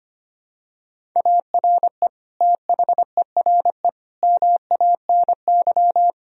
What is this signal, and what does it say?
Morse code, heard as a keyed tone, about 25 wpm, 700 Hz. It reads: ARE THERE MANY
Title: Sentences from Top 1000 Words - Review Part 1 - 25wpm